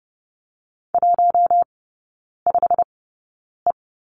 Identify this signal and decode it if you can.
Morse code, heard as a keyed tone, about 30 wpm, spaced as for 9 wpm, 700 Hz. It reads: 15E
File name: Lesson 15e - Number 5 - Three Characters 30wpm